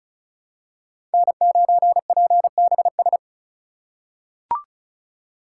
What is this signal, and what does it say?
Morse code, heard as a keyed tone, about 36 wpm, 700 Hz. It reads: N9PBS
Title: USA Call Signs - 3x Character Spacing - 35wpm